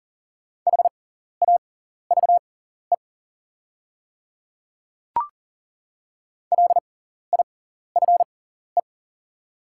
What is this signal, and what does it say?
Morse code, heard as a keyed tone, about 40 wpm, spaced as for 14 wpm, 690 Hz. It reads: HAVE LIFE
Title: 6x Character Spacing - Top 500 Words - 40wpm